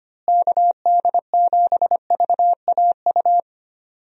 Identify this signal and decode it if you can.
Morse code, heard as a keyed tone, about 25 wpm, 700 Hz. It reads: KD7VAU
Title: USA Call Signs - 6 Characters - Repeated 2x - 25wpm